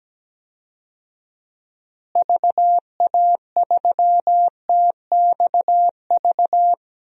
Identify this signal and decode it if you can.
Morse code, heard as a keyed tone, about 17 wpm, 700 Hz. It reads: VA3TXV
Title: DXCC-100 Call Signs - 6 Characters - Repeated 2x - 17wpm